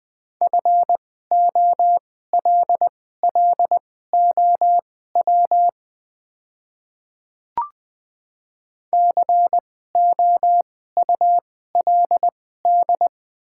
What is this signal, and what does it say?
Morse code, heard as a keyed tone, about 20 wpm, spaced as for 14 wpm, 700 Hz. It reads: FOLLOW COULD
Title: 2x Character Spacing - Top 200 Words - 20wpm